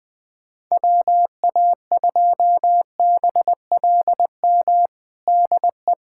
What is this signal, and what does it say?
Morse code, heard as a keyed tone, about 21 wpm, 700 Hz. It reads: WA2BLM DE
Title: USA Callsign DE Callsign 20wpm